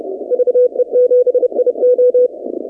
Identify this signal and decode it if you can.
Morse code, heard as a keyed tone, about 31 wpm, 500 Hz. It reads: VE7IO
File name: 2019-cw-wpx